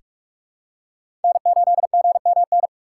Morse code, heard as a keyed tone, about 47 wpm, 700 Hz. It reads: N8GMN
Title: USA Call Signs - 6x Character Spacing - 45wpm